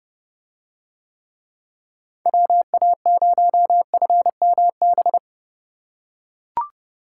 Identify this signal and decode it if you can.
Morse code, heard as a keyed tone, about 30 wpm, 700 Hz. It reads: WA0FMB